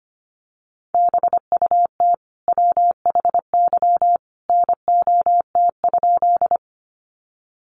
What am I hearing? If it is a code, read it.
Morse code, heard as a keyed tone, about 25 wpm, 700 Hz. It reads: BUT WHY NOT?
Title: Sentences from Top 600 Words 25wpm